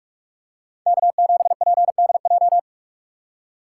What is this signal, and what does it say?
Morse code, heard as a keyed tone, about 41 wpm, 700 Hz. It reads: K7PDJ
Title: USA Call Signs - 5 Characters - 2x Character Spacing - 45wpm